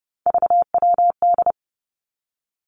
Morse code, heard as a keyed tone, about 32 wpm, 700 Hz. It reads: VWD
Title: Three Challenging Letter-Number Combinations 30wpm